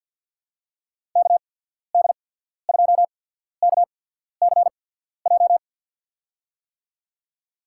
Morse code, heard as a keyed tone, about 53 wpm, spaced as for 14 wpm, 700 Hz. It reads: KD2KCJ
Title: USA Call Signs - 6 Characters - 8x Character Spacing - 50wpm